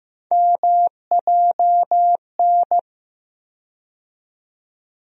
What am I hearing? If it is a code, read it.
Morse code, heard as a keyed tone, about 15 wpm, 700 Hz. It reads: MJN
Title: USA Call Sign Postfixes - 3 Letters 15wpm